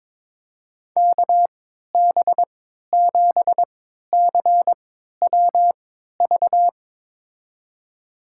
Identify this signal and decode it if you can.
Morse code, heard as a keyed tone, about 22 wpm, spaced as for 12 wpm, 700 Hz. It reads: KB7CWV
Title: USA Call Signs - 6 Characters - 3x Character Spacing - 22wpm